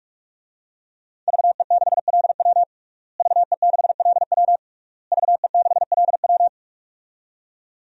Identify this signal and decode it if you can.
Morse code, heard as a keyed tone, about 46 wpm, 700 Hz. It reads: VE6LW VE6LW VE6LW
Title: DXCC-100 Call Signs - 5 Characters - Repeated 3x - 45wpm